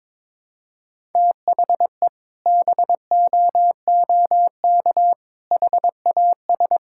Morse code, heard as a keyed tone, about 22 wpm, 700 Hz. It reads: THE BOOK HAS
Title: Sentences from Top 1000 Words - Review Part 6 - 22wpm